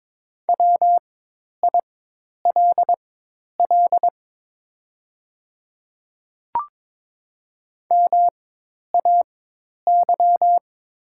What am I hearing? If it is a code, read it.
Morse code, heard as a keyed tone, about 22 wpm, spaced as for 10 wpm, 700 Hz. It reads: WILL MAY
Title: 4x Character Spacing - Top 300 Words - 22wpm